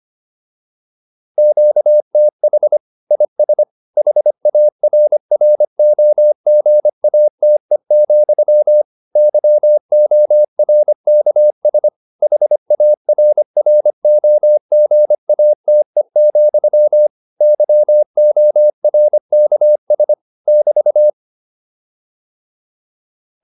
Morse code, heard as a keyed tone, about 25 wpm, 600 Hz. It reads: QTH IS HARROGATE, YORKS HARROGATE, YORKS <BT>
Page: intermediate-practice-files